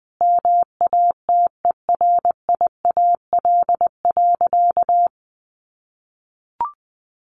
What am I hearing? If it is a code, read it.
Morse code, heard as a keyed tone, about 20 wpm, 700 Hz. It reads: MATERIAL.
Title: Sentences from Top 1100 Words 20wpm